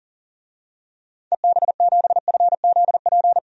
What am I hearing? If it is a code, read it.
Morse code, heard as a keyed tone, about 40 wpm, 700 Hz. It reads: EB7FZP
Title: DXCC-100 Call Signs - 6 Characters - 40wpm